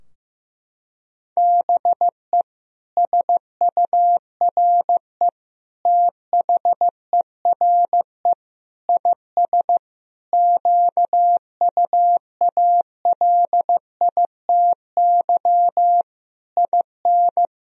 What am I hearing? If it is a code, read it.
Morse code, heard as a keyed tone, about 15 wpm, 700 Hz. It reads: BE SURE THERE IS QUALITY IN
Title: Sentences from Top 1000 Words - Review Part 4 - 15wpm